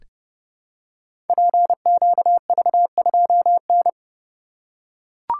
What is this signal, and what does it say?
Morse code, heard as a keyed tone, about 29 wpm, 700 Hz. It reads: PQV2N
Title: Code Groups - 5 Letter:Numbers - 30wpm